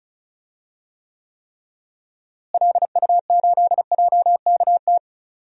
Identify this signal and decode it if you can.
Morse code, heard as a keyed tone, about 36 wpm, 700 Hz. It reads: LU8JKT
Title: DXCC-100 Call Signs - 6 Characters - Repeated 3x - 35wpm